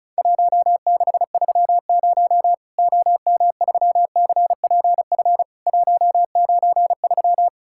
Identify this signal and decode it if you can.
Morse code, heard as a keyed tone, about 36 wpm, 700 Hz. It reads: 1630 OM3CPF 193
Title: Worked All Europe DX Contest - QTC Exchange - 35wpm